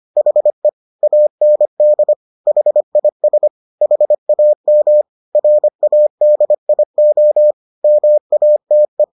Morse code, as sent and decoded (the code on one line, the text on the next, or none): .... / .- -. -.. / .... .. ... / .... .- -- / .-. .- -.. .. --- / -- .- - .
HE AND HIS HAM RADIO MATE